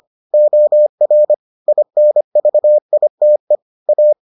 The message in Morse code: OR INVITE A

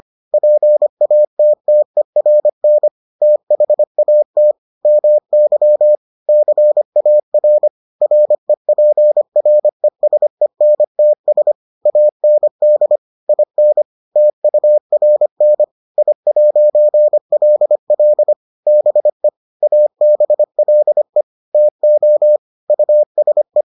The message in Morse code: PATTERN THAT MY CAR REPRESENTS AND IN TURN I'LL BE ABLE TO USE

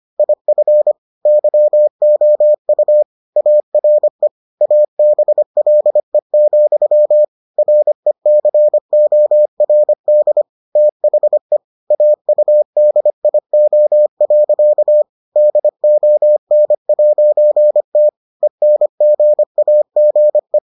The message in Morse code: IF YOU ARE ABLE, RECORD THE AUDIO. DON'T ENGAGE

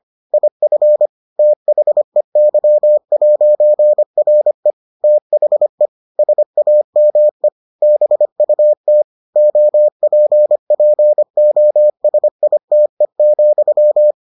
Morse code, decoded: IF THEY'RE THE SAME BUT OPPOSITE,

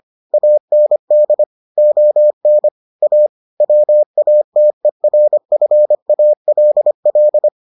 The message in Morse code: AND ON A WATERFALL

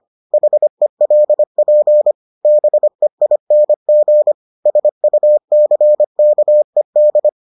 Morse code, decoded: HELP BEING SUCKED

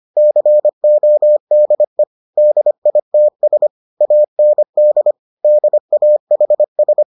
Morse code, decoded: CODE DITS AND DAHS

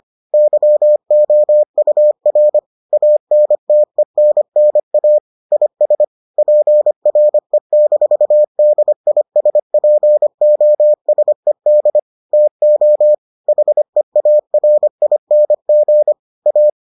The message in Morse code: YOUR ANTENNA IS PRE-DISPOSED TO HEARING A

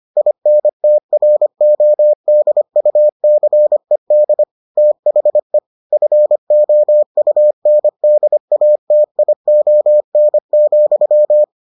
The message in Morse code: INTRODUCED THE FOUNDATION,

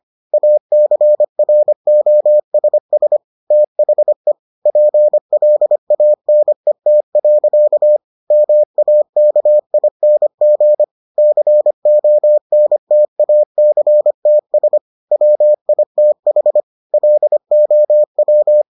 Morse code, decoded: ACROSS THE PLANET. MAKING CONTACTS WITH LOW

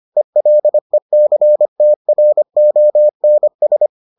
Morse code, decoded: ELECTRONS

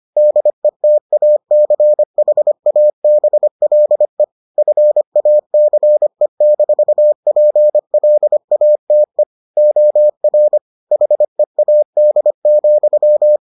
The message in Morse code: DETACHABLE FACE-PLATE OR HEAD,

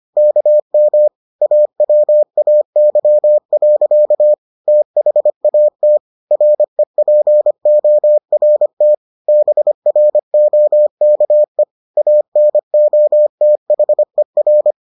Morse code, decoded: KM AWAY. THAT REPORT BROKE ANOTHER

